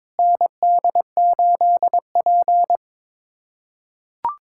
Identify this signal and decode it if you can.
Morse code, heard as a keyed tone, about 22 wpm, 700 Hz. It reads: ND8P
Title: USA Callsign DE Callsign - 4 Characters - 22wpm